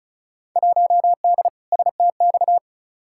Morse code, heard as a keyed tone, about 36 wpm, 700 Hz. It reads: 1D STX
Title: Field Day Exchanges - 35wpm